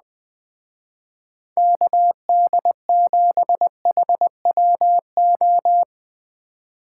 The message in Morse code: KD7HWO

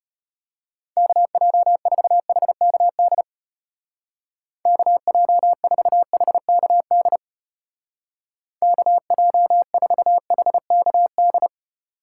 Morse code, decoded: KJ4HKD KJ4HKD KJ4HKD

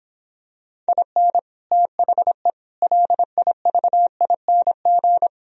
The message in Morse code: IN THE LIVING

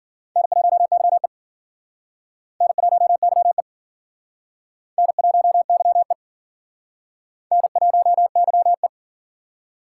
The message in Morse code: N1YE N1YE N1YE N1YE

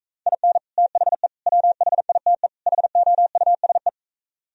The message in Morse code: IN THE WHITE HOUSE

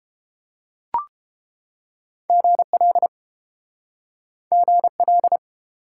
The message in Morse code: GL GL

